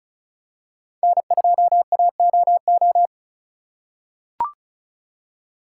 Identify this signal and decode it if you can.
Morse code, heard as a keyed tone, about 35 wpm, 700 Hz. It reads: N2AOO
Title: USA Call Signs - 5 Characters - 3x Character Spacing - 35wpm